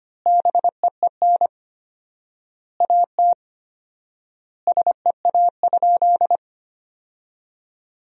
Morse code, heard as a keyed tone, about 25 wpm, 700 Hz. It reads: BEEN AT SEA?
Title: 4x Word-Spacing - Sentences from Top 300 Words - 25wpm